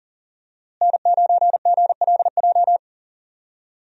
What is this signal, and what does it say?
Morse code, heard as a keyed tone, about 40 wpm, 700 Hz. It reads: N9GLJ
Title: USA Call Signs - 5 Characters - Repeated 3x - 40wpm